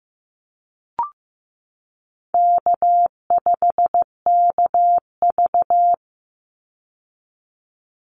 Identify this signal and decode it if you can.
Morse code, heard as a keyed tone, about 15 wpm, 700 Hz. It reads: K5KV
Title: Call Signs Repeated 2x - 15wpm